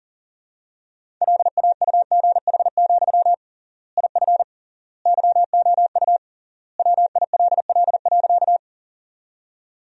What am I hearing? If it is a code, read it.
Morse code, heard as a keyed tone, about 40 wpm, 700 Hz. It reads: LAUGH, IF YOU WILL.
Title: 3x Word-Spacing - Sentences from Top 500 Words - 40wpm